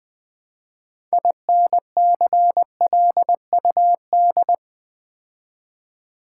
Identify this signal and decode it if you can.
Morse code, heard as a keyed tone, about 20 wpm, 700 Hz. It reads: INCLUD
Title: Build A Word - 3 Letters At A Time - 8-Letter Words - 20wpm